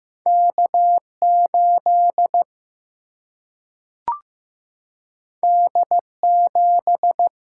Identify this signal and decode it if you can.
Morse code, heard as a keyed tone, about 15 wpm, 700 Hz. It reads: K8 D7
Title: Single-Letter Single-Number 15wpm